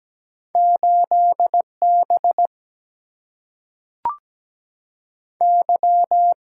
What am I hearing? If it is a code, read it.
Morse code, heard as a keyed tone, about 17 wpm, 700 Hz. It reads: 8 B Y
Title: Single-Letter Single-Number 17wpm